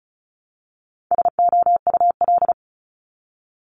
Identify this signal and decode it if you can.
Morse code, heard as a keyed tone, about 36 wpm, 700 Hz. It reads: SOUL
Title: Multi-letter Phonemes Repeated 3x 35wpm